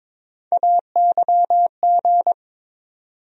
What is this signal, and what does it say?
Morse code, heard as a keyed tone, about 22 wpm, 700 Hz. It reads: AYG
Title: USA Call Sign Postfixes - 3 Letters 22wpm